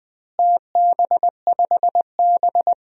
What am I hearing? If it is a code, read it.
Morse code, heard as a keyed tone, about 20 wpm, 700 Hz. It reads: TB5B
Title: DXCC-100 Call Signs - 4 Characters - Repeated 2x - 20wpm